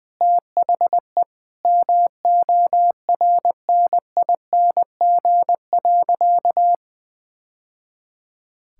Morse code, heard as a keyed tone, about 20 wpm, 700 Hz. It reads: THE MORNING.
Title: Sentences from Top 1000 Words - Review Part 6 - 20wpm